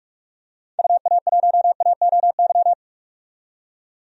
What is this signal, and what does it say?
Morse code, heard as a keyed tone, about 46 wpm, 700 Hz. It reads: UA1AOY
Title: DXCC-100 Call Signs - 6 Characters - 45wpm